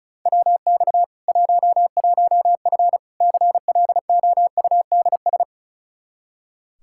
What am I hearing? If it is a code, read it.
Morse code, heard as a keyed tone, about 36 wpm, 700 Hz. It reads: WX 11F CLOUDS
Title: Practice QSO - Exchange 2 - Info - 35wpm